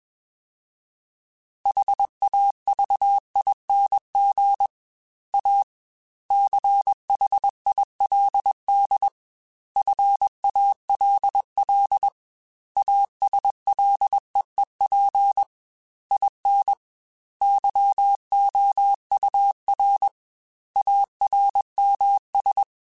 Morse code, heard as a keyed tone, about 21 wpm, 800 Hz. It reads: HAVING A CHILD FALL ASLEEP IN YOUR ARMS
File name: Quotes / Quotes-mp3s / ChildAsleep @20